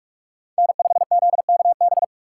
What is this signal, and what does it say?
Morse code, heard as a keyed tone, about 44 wpm, 700 Hz. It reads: N5ZKB
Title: USA Call Signs - 5 Characters - 3x Character Spacing - 45wpm